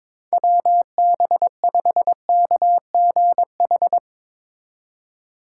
Morse code, heard as a keyed tone, about 22 wpm, 700 Hz. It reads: WB5KGH
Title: Speed Racing - USA 2x3 Call Signs 22wpm